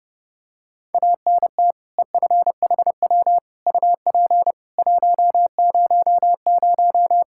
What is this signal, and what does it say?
Morse code, heard as a keyed tone, about 32 wpm, 700 Hz. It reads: ANT EFHW UP 100
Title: Practice QSO - Exchange 2 - Info - 30wpm